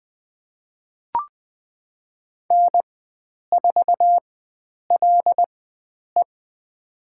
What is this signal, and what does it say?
Morse code, heard as a keyed tone, about 21 wpm, spaced as for 9 wpm, 700 Hz. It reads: N4LE
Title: USA Call Signs - 4 Characters - 4x Character Spacing - 20wpm